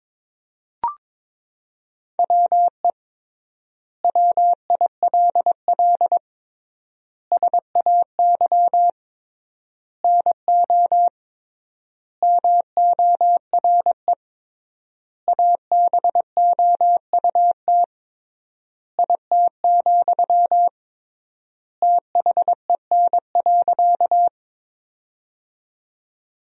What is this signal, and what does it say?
Morse code, heard as a keyed tone, about 22 wpm, 700 Hz. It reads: WE WILL SAY NO MORE ABOUT IT, THEN.
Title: 3x Word-Spacing - Sentences from Top 200 Words - 22wpm